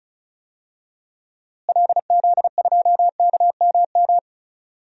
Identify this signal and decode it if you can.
Morse code, heard as a keyed tone, about 36 wpm, 700 Hz. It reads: LZ2KMM
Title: DXCC-100 Call Signs - 6 Characters - Repeated 2x - 35wpm